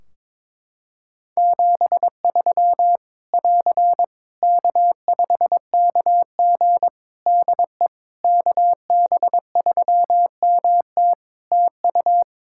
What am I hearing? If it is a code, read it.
Morse code, heard as a keyed tone, about 22 wpm, 700 Hz. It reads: 73 <AR> K5KG DE KB3MT TU
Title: K1USN SST QSO Practice 22wpm